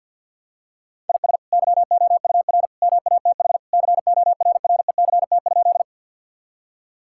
Morse code, heard as a keyed tone, about 50 wpm, 700 Hz. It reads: IS YOUR MATH CORRECT?
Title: Sentences from Top 1000 Words - Review Part 6 - 50wpm